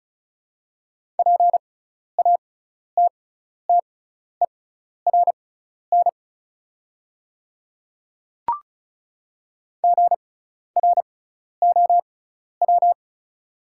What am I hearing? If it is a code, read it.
Morse code, heard as a keyed tone, about 36 wpm, spaced as for 12 wpm, 690 Hz. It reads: PATTERN GROW
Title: 6x Character Spacing - Top 500 Words - 35wpm